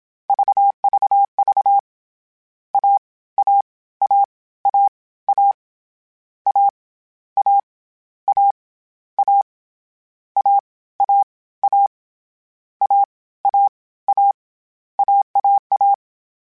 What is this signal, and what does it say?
Morse code, heard as a keyed tone, about 27 wpm, 800 Hz. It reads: VVV A A A A A A A A A A A A A A A AAA
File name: Difficult Combos / Difficult Combos-mp3s / AWJ @25